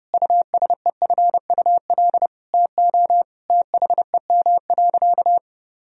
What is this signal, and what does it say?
Morse code, heard as a keyed tone, about 32 wpm, 700 Hz. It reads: USEFUL TO THEM.
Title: Sentences from Top 1000 Words - Review Part 4 - 30wpm